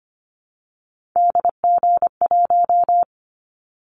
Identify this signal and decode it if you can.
Morse code, heard as a keyed tone, about 25 wpm, 700 Hz. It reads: DG1
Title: Callsign Prefixes for Top 25 DXCC - Repeated 3x - 25wpm